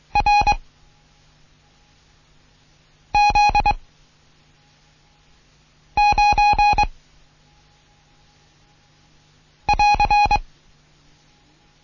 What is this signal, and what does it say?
Morse code, heard as a keyed tone, about 23 wpm, 830 Hz. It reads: R Z 9 <AR>
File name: Morsen / Kurs A 16